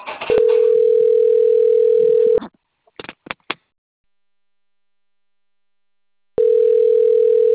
Ringback tone (North America), 440 + 480 Hz.
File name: rg-1600-unknown- - - .35361